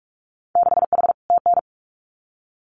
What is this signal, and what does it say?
Morse code, heard as a keyed tone, about 44 wpm, 700 Hz. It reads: 6H TN
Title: Winter Field Day Exchanges Encore 45wpm